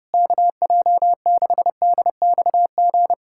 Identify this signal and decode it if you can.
Morse code, heard as a keyed tone, about 32 wpm, 700 Hz. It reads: KJ6DXG
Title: USA Callsign DE Callsign 30wpm